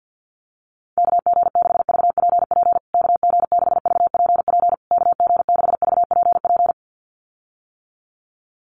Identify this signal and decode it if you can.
Morse code, heard as a keyed tone, about 50 wpm, 700 Hz. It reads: KG6VPP KG6VPP KG6VPP
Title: USA Call Signs Repeated 3x 50wpm